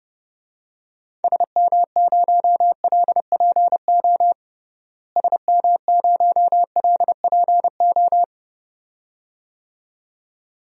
Morse code, heard as a keyed tone, about 30 wpm, 700 Hz. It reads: SM0LPO SM0LPO
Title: DXCC-100 Call Signs - 6 Characters - Repeated 2x - 30wpm